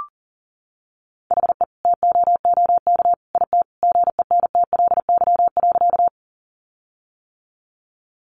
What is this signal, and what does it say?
Morse code, heard as a keyed tone, about 40 wpm, 700 Hz. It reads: HE TOOK IT GENTLY.
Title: Sentences from Top 1500 Words 40wpm